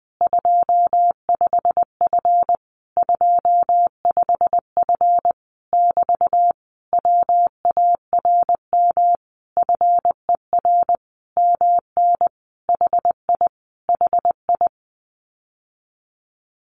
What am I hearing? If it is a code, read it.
Morse code, heard as a keyed tone, about 20 wpm, 700 Hz. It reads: 25F 25F <BT> WARM FER MN HI HI
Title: Common QSO Phrases Encore 20wpm